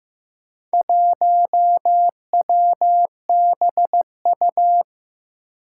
Morse code, heard as a keyed tone, about 15 wpm, 700 Hz. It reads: 1WBU
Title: Four Challenging Letter-Number Combinations 15wpm